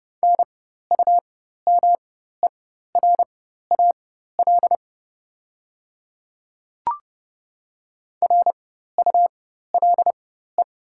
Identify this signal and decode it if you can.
Morse code, heard as a keyed tone, about 30 wpm, spaced as for 14 wpm, 690 Hz. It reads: NUMERAL RULE
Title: 4x Character Spacing - Top 500 Words - 30wpm